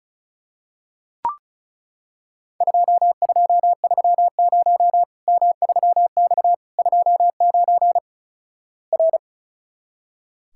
Morse code, heard as a keyed tone, about 36 wpm, 700 Hz. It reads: 2230 M3X 29 R
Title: Worked All Europe DX Contest - QTC Exchange - 35wpm